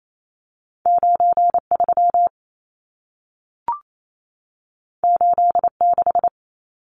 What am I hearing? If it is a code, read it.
Morse code, heard as a keyed tone, about 28 wpm, 700 Hz. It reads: 93 86
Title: Numbers - 2-Digit 28wpm